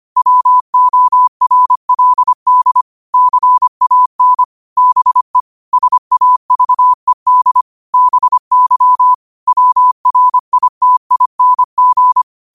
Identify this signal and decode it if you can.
Morse code, heard as a keyed tone, about 25 wpm, 1000 Hz. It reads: WORLD CAN BE SAVED BY WRITING
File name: qotdfiles / 25wpm